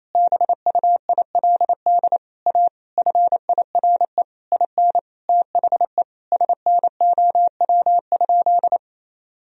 Morse code, heard as a keyed tone, about 27 wpm, 700 Hz. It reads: BUILD A FIRE IN THE SNOW?
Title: Sentences from Top 1000 Words - Review Part 3 - 28wpm